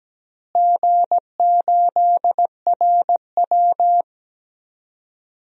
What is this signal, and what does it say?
Morse code, heard as a keyed tone, about 17 wpm, 700 Hz. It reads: G8RW
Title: DXCC-100 Call Signs - 4 Characters - Repeated 3x - 17wpm